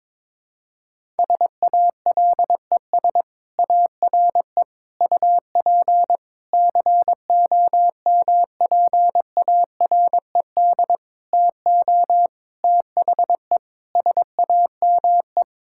Morse code, heard as a keyed tone, about 22 wpm, 700 Hz. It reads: SALES ARE UP COMPARED TO THE SAME
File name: Sentences with 7-Word Phrases - 22wpm